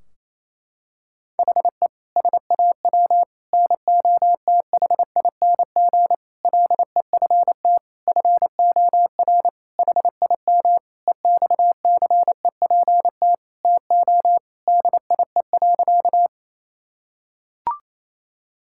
Morse code, heard as a keyed tone, about 28 wpm, 700 Hz. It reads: HE SAW NOTHING LEFT FOR HIM EXCEPT TO DIE.
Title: Sentences from Top 1600 Words 28wpm